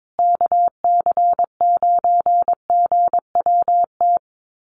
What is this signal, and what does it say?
Morse code, heard as a keyed tone, about 22 wpm, 700 Hz. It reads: KC9GWT
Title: USA Call Signs Repeated 3x 22wpm